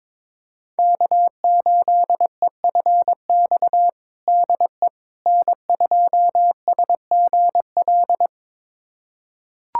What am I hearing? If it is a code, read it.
Morse code, heard as a keyed tone, about 21 wpm, 700 Hz. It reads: K8EFX DE N2SGL
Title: USA Callsign DE Callsign - 5 Characters - 22wpm